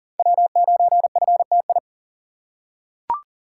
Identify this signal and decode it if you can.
Morse code, heard as a keyed tone, about 37 wpm, 700 Hz. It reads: W9FTI
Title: USA Call Signs - 5 Characters - 3x Character Spacing - 40wpm